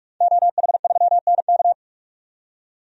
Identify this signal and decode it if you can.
Morse code, heard as a keyed tone, about 45 wpm, 700 Hz. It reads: OH3NK
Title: DXCC-100 Call Signs - 5 Characters - 45wpm